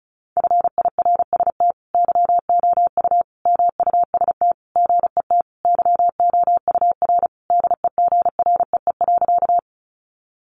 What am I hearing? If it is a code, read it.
Morse code, heard as a keyed tone, about 34 wpm, 700 Hz. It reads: FIRST YOU MUST GET YOUR DEGREE.
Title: Sentences from Top 1500 Words 35wpm